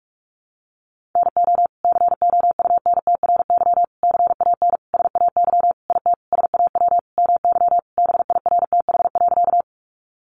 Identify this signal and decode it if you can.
Morse code, heard as a keyed tone, about 44 wpm, 700 Hz. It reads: NO COUNTRY CAN SAY IT SAW MY BIRTH.
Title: Sentences from Top 800 Words 45wpm